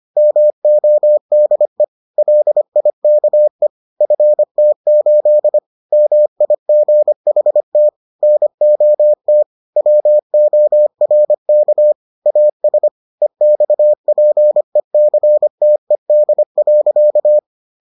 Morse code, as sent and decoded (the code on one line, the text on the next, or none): -- --- -.. / .-.. .. -.- . / ..-. - ---.. / -- .. --. .... - / -. --- - / .-- --- .-. -.- / .- ... / . -..- .--. -.-. - . -.. .-.-.-
MODE LIKE FT8 MIGHT NOT WORK AS EXPECTED.